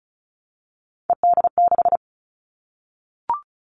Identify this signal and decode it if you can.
Morse code, heard as a keyed tone, about 34 wpm, 700 Hz. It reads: ED6
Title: Callsign Prefixes for Top 50 DXCC - 35wpm